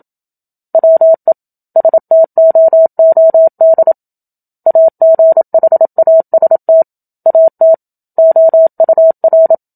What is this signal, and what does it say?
Morse code, heard as a keyed tone, about 27 wpm, 650 Hz. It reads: WE STOOD AGHAST AT OUR
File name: Book Two / 01-Chapter 1 - 27 WPM